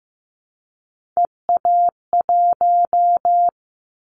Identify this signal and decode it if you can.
Morse code, heard as a keyed tone, about 15 wpm, 700 Hz. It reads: EA1
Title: Callsign Prefixes for Top 25 DXCC - 15wpm